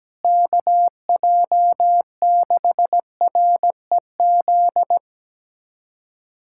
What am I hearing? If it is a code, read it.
Morse code, heard as a keyed tone, about 17 wpm, 700 Hz. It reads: KJ6REZ
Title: USA Callsign DE Callsign - 6 Characters - 17wpm